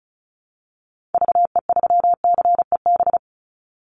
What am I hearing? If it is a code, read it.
Morse code, heard as a keyed tone, about 34 wpm, 700 Hz. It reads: VE3CEB
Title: Call Signs - 35wpm